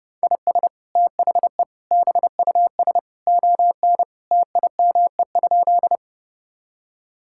Morse code, heard as a keyed tone, about 28 wpm, 700 Hz. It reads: IS THE BUS ON TIME?
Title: Sentences from Top 1000 Words - Review Part 6 - 30wpm